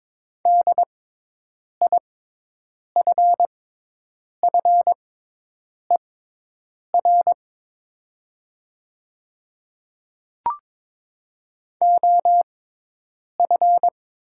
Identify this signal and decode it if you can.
Morse code, heard as a keyed tone, about 22 wpm, spaced as for 8 wpm, 700 Hz. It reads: DIFFER OF
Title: 6x Character Spacing - Top 200 Words - 22wpm